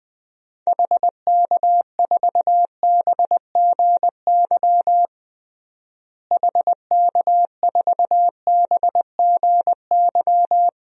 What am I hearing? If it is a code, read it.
Morse code, heard as a keyed tone, about 21 wpm, 700 Hz. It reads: HK4BGY HK4BGY
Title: DXCC-100 Call Signs - 6 Characters - Repeated 3x - 20wpm